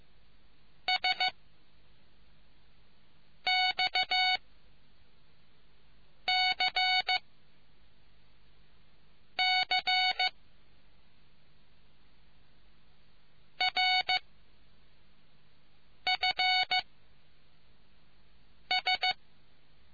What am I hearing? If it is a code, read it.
Morse code, heard as a keyed tone, about 15 wpm, 740 Hz. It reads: S X C C R F S